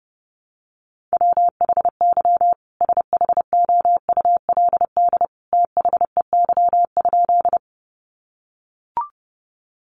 Morse code, heard as a keyed tone, about 30 wpm, 700 Hz. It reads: WHY SHOULD THEY?
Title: Sentences from Top 300 Words 30wpm